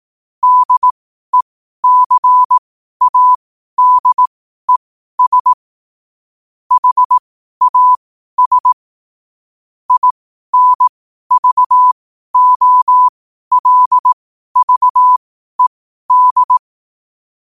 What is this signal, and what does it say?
Morse code, heard as a keyed tone, about 18 wpm, spaced as for 12 wpm, 1000 Hz. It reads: DECADES HAS INVOLVED
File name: qotdfiles / 12wpm